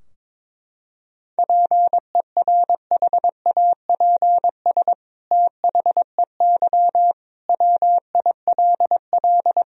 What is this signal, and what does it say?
Morse code, heard as a keyed tone, about 22 wpm, 700 Hz. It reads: PERHAPS THEY WILL
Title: Sentences from Top 1000 Words - Review Part 2 - 22wpm